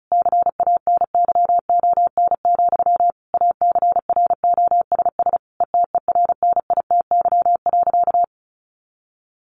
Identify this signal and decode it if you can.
Morse code, heard as a keyed tone, about 34 wpm, 700 Hz. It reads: CANYON, ACROSS ETERNITY.
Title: A Sound Of Thunder 35wpm